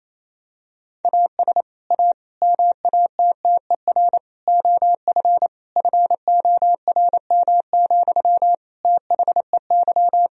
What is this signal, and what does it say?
Morse code, heard as a keyed tone, about 28 wpm, 700 Hz. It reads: AS A MATTER OF FORM, THEY
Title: Sentences from Top 1000 Words - Review Part 1 - 28wpm